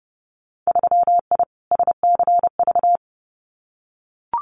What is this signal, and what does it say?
Morse code, heard as a keyed tone, about 30 wpm, 700 Hz. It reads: 3I SCV
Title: Winter Field Day Exchanges Encore 30wpm